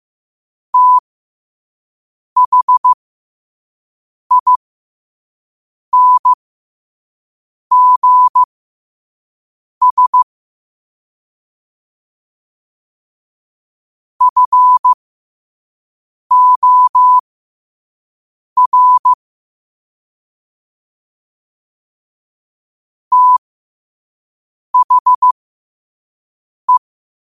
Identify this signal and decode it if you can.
Morse code, heard as a keyed tone, about 15 wpm, spaced as for 5 wpm, 1000 Hz. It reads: THINGS FOR THE
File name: qotdfiles / 5wpm